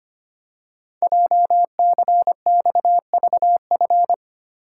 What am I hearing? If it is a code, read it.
Morse code, heard as a keyed tone, about 25 wpm, 700 Hz. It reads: JCXVF
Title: Code Groups - 5 Letters - 25wpm